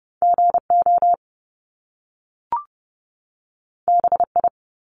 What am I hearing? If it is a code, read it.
Morse code, heard as a keyed tone, about 30 wpm, 700 Hz. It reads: G O B I
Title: Top Two Letter Combinations in Words 30wpm